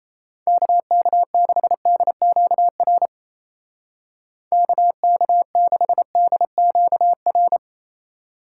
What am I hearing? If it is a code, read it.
Morse code, heard as a keyed tone, about 37 wpm, 700 Hz. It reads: KK6DQR KK6DQR